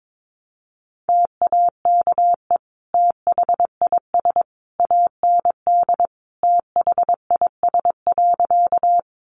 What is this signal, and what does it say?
Morse code, heard as a keyed tone, about 22 wpm, 700 Hz. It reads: TAKE THIS AND THIS.
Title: Sentences from Top 200 Words 22wpm